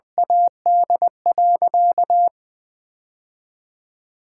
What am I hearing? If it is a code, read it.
Morse code, heard as a keyed tone, about 20 wpm, 700 Hz. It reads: AD.
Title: Sentences with 2-Word Phrases - 20wpm